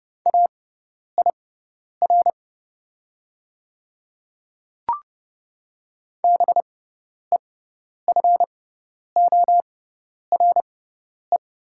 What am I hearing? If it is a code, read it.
Morse code, heard as a keyed tone, about 30 wpm, spaced as for 10 wpm, 700 Hz. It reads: AIR BEFORE
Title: 6x Character Spacing - Top 200 Words - 30wpm